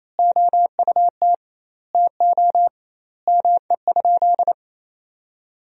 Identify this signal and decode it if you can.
Morse code, heard as a keyed tone, about 28 wpm, 700 Hz. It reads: OUT TO ME?
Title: 2x Word-Spacing - Sentences from Top 200 Words - 28wpm